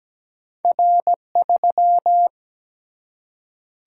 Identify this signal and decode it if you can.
Morse code, heard as a keyed tone, about 17 wpm, 700 Hz. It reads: R3
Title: Single-Letter Single-Number 17wpm